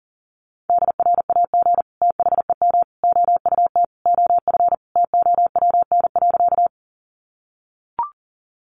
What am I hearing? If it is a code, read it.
Morse code, heard as a keyed tone, about 40 wpm, 700 Hz. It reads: DRAG THEM OUT OF TOWN.
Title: Sentences from Top 1500 Words 40wpm